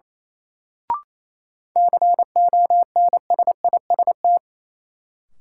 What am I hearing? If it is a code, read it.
Morse code, heard as a keyed tone, about 28 wpm, 700 Hz. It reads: CONSIST
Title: Sentences from Top 1400 Words - New Word Drill Alt - 28wpm